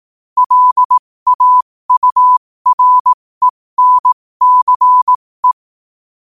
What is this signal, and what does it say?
Morse code, heard as a keyed tone, about 18 wpm, spaced as for 15 wpm, 1000 Hz. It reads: LAURENCE
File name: qotdfiles / 15wpm